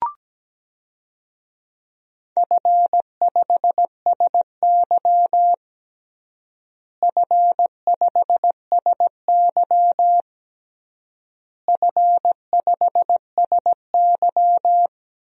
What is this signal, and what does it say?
Morse code, heard as a keyed tone, about 17 wpm, 700 Hz. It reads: F5SY F5SY F5SY
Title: DXCC-100 Call Signs - 4 Characters - Repeated 3x - 17wpm